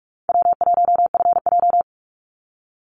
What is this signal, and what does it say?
Morse code, heard as a keyed tone, about 44 wpm, 700 Hz. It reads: W1FJ
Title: Easy Call Signs Repeated 2x - 45wpm